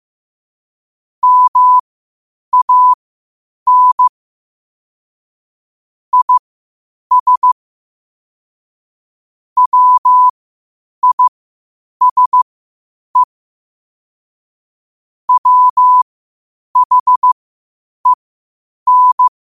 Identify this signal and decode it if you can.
Morse code, heard as a keyed tone, about 15 wpm, spaced as for 8 wpm, 1000 Hz. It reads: MAN IS WISE WHEN
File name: qotdfiles / 8wpm